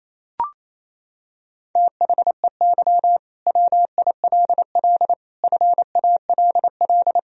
Morse code, heard as a keyed tone, about 28 wpm, 700 Hz. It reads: THEY WILL FALL
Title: Sentences from Top 1000 Words - Review Part 4 - 28wpm